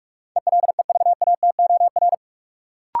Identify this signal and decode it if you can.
Morse code, heard as a keyed tone, about 44 wpm, 700 Hz. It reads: ELEVATOR
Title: Sentences from Top 1600 Words - New Word Drill - 45wpm